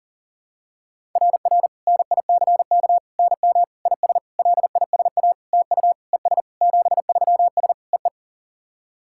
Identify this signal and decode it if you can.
Morse code, heard as a keyed tone, about 37 wpm, 700 Hz. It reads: RR NICK NM IS LISA TU ES 73S EE
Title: SOTA QSO Practice 40wpm